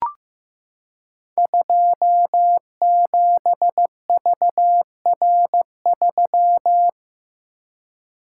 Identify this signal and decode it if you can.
Morse code, heard as a keyed tone, about 15 wpm, 700 Hz. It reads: 27VR3
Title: Code Groups - 5 Letter:Numbers - 15wpm